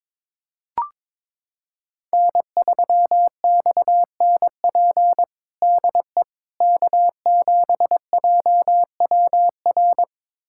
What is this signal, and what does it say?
Morse code, heard as a keyed tone, about 22 wpm, 700 Hz. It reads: N3XNP DE K7JWR
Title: USA Callsign DE Callsign - 5 Characters - 22wpm